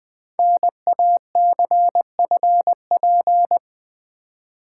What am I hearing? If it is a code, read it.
Morse code, heard as a keyed tone, about 20 wpm, 700 Hz. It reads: NACFP
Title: Code Groups - 5 Letter:Numbers - 20wpm